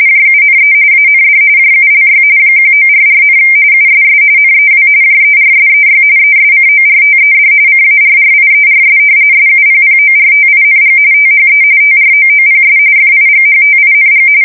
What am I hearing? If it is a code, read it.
RTTY, about 45.45 baud. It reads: JUNE AND JULY 1944, USS NEW JERSEY ESCORTED CARRIERS SUPPORTING THE TROOPS FIGHTING
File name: fsk170 / battleshipnj